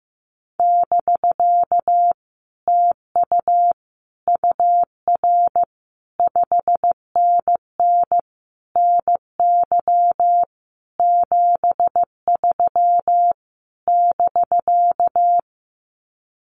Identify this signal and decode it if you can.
Morse code, heard as a keyed tone, about 15 wpm, 700 Hz. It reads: <BK> TU UR 5NN NY 73 <BK>
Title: Common QSO Phrases Encore 15wpm